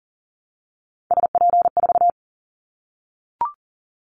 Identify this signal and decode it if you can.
Morse code, heard as a keyed tone, about 40 wpm, 700 Hz. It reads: SP4
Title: Callsign Prefixes for Top 25 DXCC - 40wpm